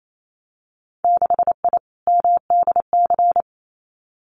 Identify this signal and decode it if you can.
Morse code, heard as a keyed tone, about 28 wpm, 700 Hz. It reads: 6I MDC
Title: Winter Field Day Exchanges Encore 28wpm